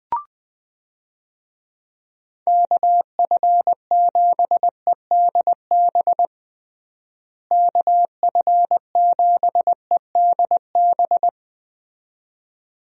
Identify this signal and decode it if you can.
Morse code, heard as a keyed tone, about 21 wpm, 700 Hz. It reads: KF7EDB KF7EDB
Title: USA Call Signs - 6 Characters - Repeated 2x - 20wpm